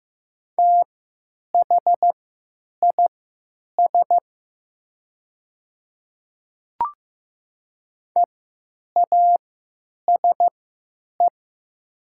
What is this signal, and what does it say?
Morse code, heard as a keyed tone, about 15 wpm, spaced as for 9 wpm, 700 Hz. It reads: THIS EASE
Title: 3x Character Spacing - Top 300 Words - 15wpm